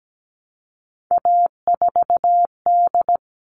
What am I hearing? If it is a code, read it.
Morse code, heard as a keyed tone, about 17 wpm, 700 Hz. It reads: A4D
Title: All Cut Numbers - 3-Digit 17wpm